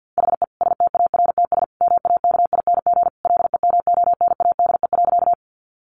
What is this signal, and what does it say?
Morse code, heard as a keyed tone, about 50 wpm, 700 Hz. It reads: HE STARTS MAKING LEMONADE.
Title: Additional Easy Phrases 50wpm